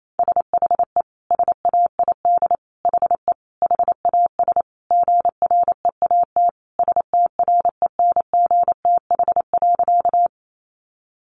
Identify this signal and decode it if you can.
Morse code, heard as a keyed tone, about 28 wpm, 700 Hz. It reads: SHE SAID HE HAS GREAT STRENGTH.
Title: Sentences from Top 1300 Words 28wpm